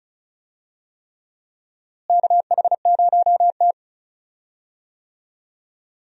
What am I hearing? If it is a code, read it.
Morse code, heard as a keyed tone, about 35 wpm, 700 Hz. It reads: KH0T
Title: USA Call Signs - 4 Characters - Repeated 3x - 35wpm